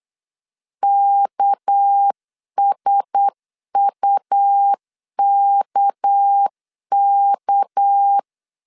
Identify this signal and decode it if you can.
Morse code, heard as a keyed tone, about 8 wpm, 800 Hz. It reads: KSUKK
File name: CW / DET / 1 6 1500C S C 8wpm 5219